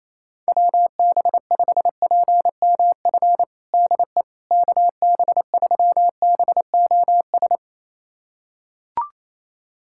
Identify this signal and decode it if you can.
Morse code, heard as a keyed tone, about 28 wpm, 700 Hz. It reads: WB5PMF DE KB3BOS
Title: USA Callsign DE Callsign 28wpm